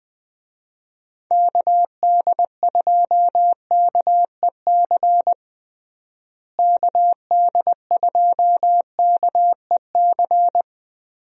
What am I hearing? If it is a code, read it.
Morse code, heard as a keyed tone, about 21 wpm, 700 Hz. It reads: KD2KEC KD2KEC
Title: USA Call Signs - 6 Characters - Repeated 2x - 20wpm